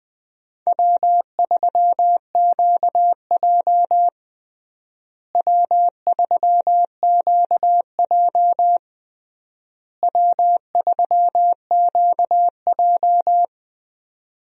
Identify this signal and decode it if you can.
Morse code, heard as a keyed tone, about 20 wpm, 700 Hz. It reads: W3QJ W3QJ W3QJ
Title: USA Call Signs - 4 Characters - Repeated 3x - 20wpm